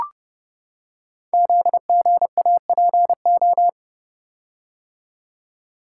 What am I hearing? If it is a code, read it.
Morse code, heard as a keyed tone, about 30 wpm, 700 Hz. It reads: ZGAPO
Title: Code Groups - 5 Letters - 30wpm